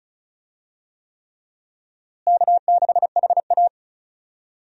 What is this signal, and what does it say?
Morse code, heard as a keyed tone, about 36 wpm, 700 Hz. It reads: K6HA
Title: USA Call Signs - 4 Characters - Repeated 3x - 35wpm